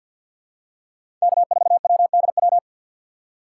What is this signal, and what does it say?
Morse code, heard as a keyed tone, about 51 wpm, 700 Hz. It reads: K4WDW
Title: USA Call Signs - 5 Characters - 2x Character Spacing - 50wpm